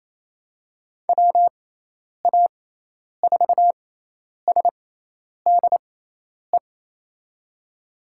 Morse code, heard as a keyed tone, about 28 wpm, spaced as for 10 wpm, 700 Hz. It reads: WA4SDE
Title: USA Call Signs - 6 Characters - 6x Character Spacing - 28wpm